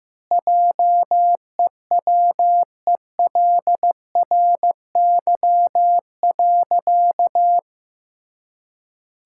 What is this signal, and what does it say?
Morse code, heard as a keyed tone, about 15 wpm, 700 Hz. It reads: JEWELRY.
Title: Pangram Challenge v2 - 15wpm